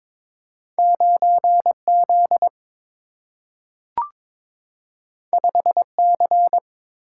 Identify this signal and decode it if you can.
Morse code, heard as a keyed tone, about 22 wpm, 700 Hz. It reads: 9 Z 5 C
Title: Single-Letter Single-Number 22wpm